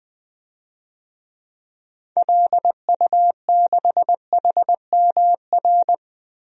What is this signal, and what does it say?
Morse code, heard as a keyed tone, about 20 wpm, 700 Hz. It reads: LU6HMR
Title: DXCC-100 Call Signs - 6 Characters - Repeated 3x - 20wpm